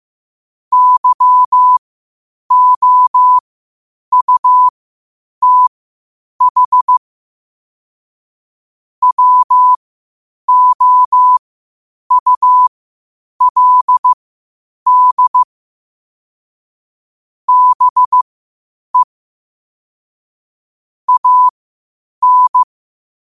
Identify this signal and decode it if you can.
Morse code, heard as a keyed tone, about 15 wpm, spaced as for 8 wpm, 1000 Hz. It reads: YOUTH WOULD BE AN
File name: qotdfiles / 8wpm